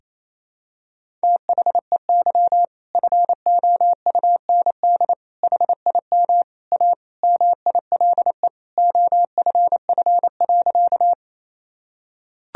Morse code, heard as a keyed tone, about 28 wpm, 700 Hz. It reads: THEY FOUND HIM A MILE OFF.
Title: Sentences from Top 1000 Words - Review Part 1 - 28wpm